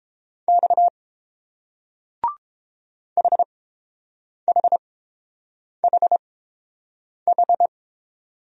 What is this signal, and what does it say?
Morse code, heard as a keyed tone, about 30 wpm, spaced as for 7 wpm, 700 Hz. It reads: X HHHH